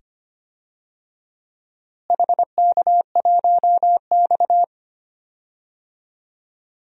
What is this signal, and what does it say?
Morse code, heard as a keyed tone, about 25 wpm, 700 Hz. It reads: HK1X
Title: DXCC-100 Call Signs - 4 Characters - Repeated 2x - 25wpm